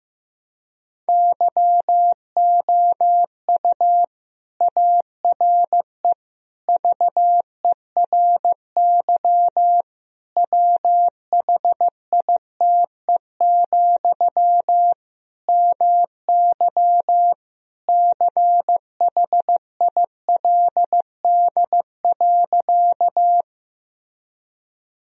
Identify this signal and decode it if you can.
Morse code, heard as a keyed tone, about 16 wpm, 700 Hz. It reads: YOU ARE VERY WHITE, MY CHILD.
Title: Sentences from Top 1000 Words - Review Part 1 - 15wpm